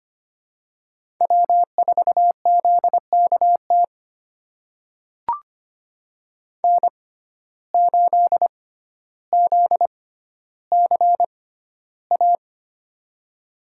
Morse code, heard as a keyed tone, about 25 wpm, spaced as for 13 wpm, 700 Hz. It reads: W4ZKT N8ZCA
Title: USA Call Signs - 5 Characters - 6x Character Spacing - 25wpm